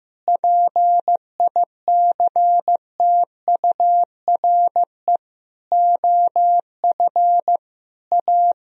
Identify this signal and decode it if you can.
Morse code, heard as a keyed tone, about 15 wpm, 700 Hz. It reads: PICTURE OF A
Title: Sentences from Top 1000 Words - Review Part 2 - 15wpm